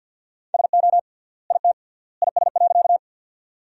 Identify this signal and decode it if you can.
Morse code, heard as a keyed tone, about 50 wpm, 700 Hz. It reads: SO IT IS.
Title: 3x Word-Spacing - Sentences from Top 100 Words - 50wpm